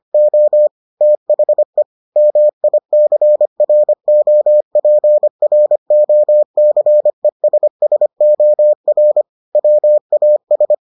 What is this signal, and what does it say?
Morse code, heard as a keyed tone, about 25 wpm, 600 Hz. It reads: O THE MICROPROCESSOR WAS